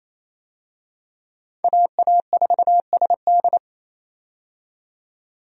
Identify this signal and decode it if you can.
Morse code, heard as a keyed tone, about 28 wpm, 700 Hz. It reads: AA4SD
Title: USA Call Signs - 5 Characters - 28wpm